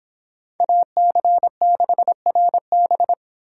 Morse code, heard as a keyed tone, about 26 wpm, 700 Hz. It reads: AC6RB